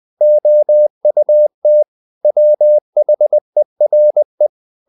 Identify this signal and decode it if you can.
Morse code, heard as a keyed tone, about 20 wpm, 600 Hz. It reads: OUT WHERE
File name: the old man and the sea / the old man and the sea part 1 20wpm